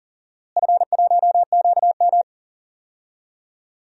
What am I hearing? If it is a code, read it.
Morse code, heard as a keyed tone, about 40 wpm, 690 Hz. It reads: F1QM
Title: DXCC-100 Call Signs - 4 Characters - Repeated 2x - 40wpm